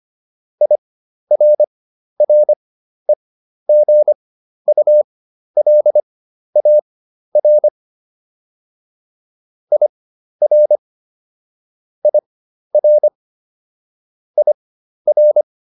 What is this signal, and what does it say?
Morse code, heard as a keyed tone, about 25 wpm, spaced as for 12 wpm, 600 Hz. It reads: IRREGULAR IR IR IR